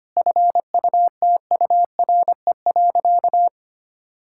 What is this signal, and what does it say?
Morse code, heard as a keyed tone, about 25 wpm, 700 Hz. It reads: FUTURE.
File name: practice / 25wpm / Difficult Sentences with Hints - 25wpm